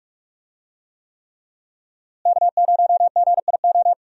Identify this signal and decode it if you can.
Morse code, heard as a keyed tone, about 41 wpm, 700 Hz. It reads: K0GIO
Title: USA Call Signs - 5 Characters - Repeated 3x - 45wpm